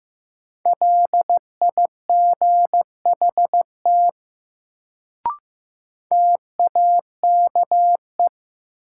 Morse code, heard as a keyed tone, about 15 wpm, 700 Hz. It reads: LIGHT TAKE
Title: Sentences from Top 200 Words - New Word Drill - 15wpm